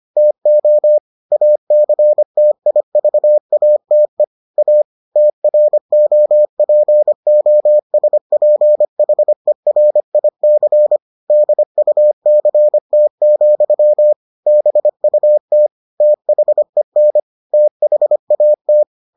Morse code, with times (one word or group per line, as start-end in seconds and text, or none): 0.2-1.0 TO
1.3-4.2 ACTIVATE
4.6-4.8 A
5.2-11.0 TROPOSPHERIC
11.3-14.1 DUCT,
14.5-15.7 BUT
16.0-17.2 THEN
17.5-18.8 THAT